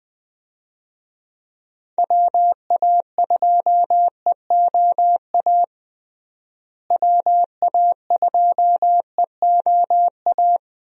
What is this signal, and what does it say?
Morse code, heard as a keyed tone, about 20 wpm, 700 Hz. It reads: WA2EOA WA2EOA
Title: USA Call Signs - 6 Characters - Repeated 3x - 20wpm